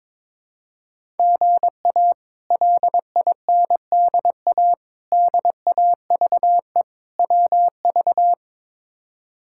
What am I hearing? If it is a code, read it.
Morse code, heard as a keyed tone, about 22 wpm, 700 Hz. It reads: GA LINDA DAVE WV
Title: K1USN SST QSO Practice - 22wpm